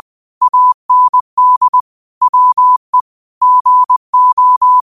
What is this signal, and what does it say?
Morse code, heard as a keyed tone, about 20 wpm, 1000 Hz. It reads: AND WE GO